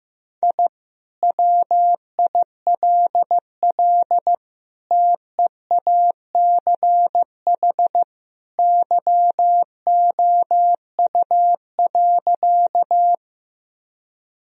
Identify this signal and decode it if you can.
Morse code, heard as a keyed tone, about 15 wpm, 700 Hz. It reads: I WILL TEACH YOU.
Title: Sentences from Top 1000 Words - Review Part 2 - 15wpm